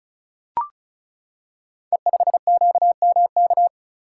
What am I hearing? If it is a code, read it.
Morse code, heard as a keyed tone, about 36 wpm, 690 Hz. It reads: E5QMK
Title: Code Groups - 5 Letter:Numbers - 35wpm